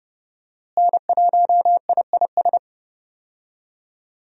Morse code, heard as a keyed tone, about 30 wpm, 700 Hz. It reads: N1IIS
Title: USA Call Signs - 5 Characters - Repeated 3x - 30wpm